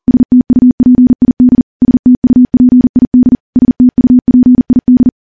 Morse code, heard as a keyed tone, about 40 wpm, 250 Hz. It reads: STUPID STUPID STUPID
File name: morse / morse-training-40WPM